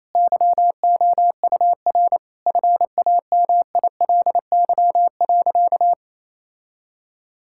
Morse code, heard as a keyed tone, about 28 wpm, 700 Hz. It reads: YOUR FAMILY.
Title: Sentences from Top 1600 Words 28wpm